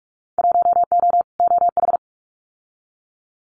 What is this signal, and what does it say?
Morse code, heard as a keyed tone, about 44 wpm, 700 Hz. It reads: 1O OH
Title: Winter Field Day Exchanges 45wpm